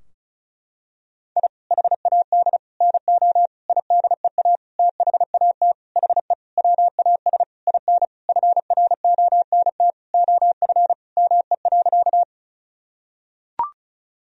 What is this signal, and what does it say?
Morse code, heard as a keyed tone, about 34 wpm, 700 Hz. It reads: I HAD NO IDEA THAT HE WAS IN FRONT OF ME.
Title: Sentences from Top 1000 Words - Review Part 2 - 35wpm